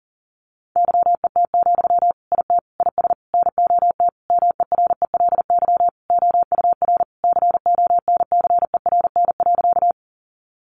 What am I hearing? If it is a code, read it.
Morse code, heard as a keyed tone, about 40 wpm, 700 Hz. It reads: YET, IT IS NOT MERELY OUR CONCERN.
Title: Sentences from Top 1400 Words 40wpm